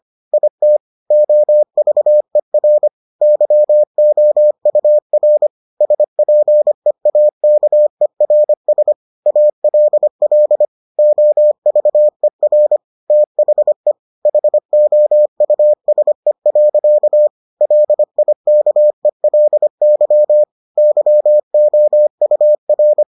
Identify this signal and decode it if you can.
Morse code, heard as a keyed tone, about 25 wpm, 600 Hz. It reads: IT OVER YOUR SPEAKERS ALL OVER THE HOUSE. LIKELY YOUR